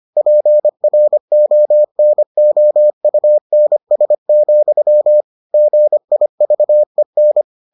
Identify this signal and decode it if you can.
Morse code, heard as a keyed tone, about 25 wpm, 600 Hz. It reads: PRONOUNS, GIVEN